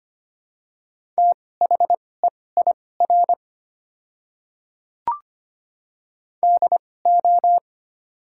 Morse code, heard as a keyed tone, about 25 wpm, spaced as for 18 wpm, 700 Hz. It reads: THEIR DO
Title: 2x Character Spacing - Top 300 Words - 25wpm